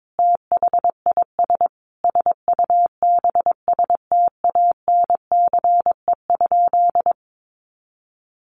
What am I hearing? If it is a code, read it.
Morse code, heard as a keyed tone, about 22 wpm, 700 Hz. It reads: THIS SUBSTANCE?
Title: Sentences from Top 1500 Words 22wpm